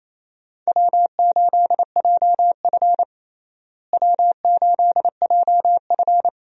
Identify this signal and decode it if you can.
Morse code, heard as a keyed tone, about 27 wpm, 700 Hz. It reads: W8JF W8JF
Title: USA Call Signs - 4 Characters - Repeated 2x - 28wpm